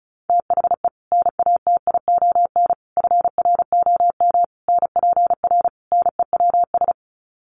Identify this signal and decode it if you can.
Morse code, heard as a keyed tone, about 34 wpm, 700 Hz. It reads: THE NATION FROM NPR NEWS
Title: Sets of 5 Words 35wpm